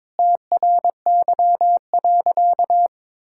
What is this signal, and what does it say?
Morse code, heard as a keyed tone, about 22 wpm, 700 Hz. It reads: TRY.
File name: practice / 22wpm / Sentences with 2-Word Phrases - 22wpm